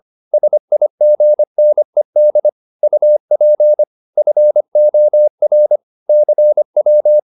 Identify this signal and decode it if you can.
Morse code, heard as a keyed tone, about 25 wpm, 600 Hz. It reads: SIGNED UP FOR CW